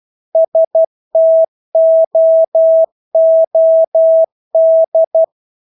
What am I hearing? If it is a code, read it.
Morse code, heard as a keyed tone, about 12 wpm, 650 Hz. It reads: STOOD
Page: Engelske ord 12 wpm